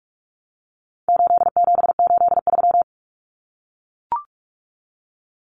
Morse code, heard as a keyed tone, about 44 wpm, 700 Hz. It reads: 8783
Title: Numbers - 4-Digit 45wpm